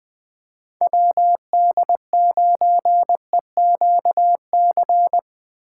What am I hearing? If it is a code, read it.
Morse code, heard as a keyed tone, about 20 wpm, 700 Hz. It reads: WD9EQC
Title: USA Call Signs - 6 Characters - 8x Character Spacing - 20wpm